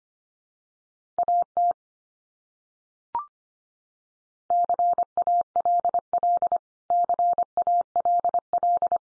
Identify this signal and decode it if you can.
Morse code, heard as a keyed tone, about 25 wpm, 700 Hz. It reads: AT CALL CALL
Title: Top 100 Words 25wpm x3